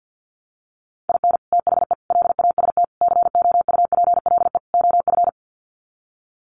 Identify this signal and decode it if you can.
Morse code, heard as a keyed tone, about 50 wpm, 700 Hz. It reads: IN THE LAST COUPLE OF
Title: Sets of 5 Words 50wpm